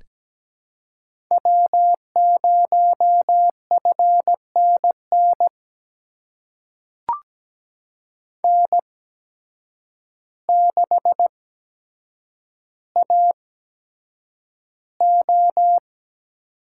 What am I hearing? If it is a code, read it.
Morse code, heard as a keyed tone, about 17 wpm, spaced as for 8 wpm, 700 Hz. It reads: W0FNN N6AO
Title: USA Call Signs - 5 Characters - 8x Character Spacing - 17wpm